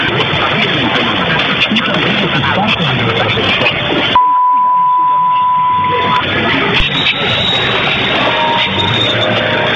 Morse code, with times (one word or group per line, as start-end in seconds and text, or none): none